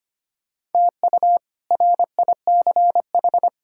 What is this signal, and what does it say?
Morse code, heard as a keyed tone, about 25 wpm, 700 Hz. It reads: TU RICH
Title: K1USN SST QSO Practice - 25wpm